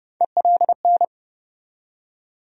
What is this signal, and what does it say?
Morse code, heard as a keyed tone, about 30 wpm, 700 Hz. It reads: ELN
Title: USA Call Sign Postfixes - 3 Letters 30wpm